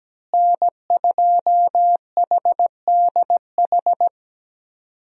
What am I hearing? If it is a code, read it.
Morse code, heard as a keyed tone, about 17 wpm, 700 Hz. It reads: N2HDH
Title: USA Call Signs - 5 Characters - Repeated 2x - 17wpm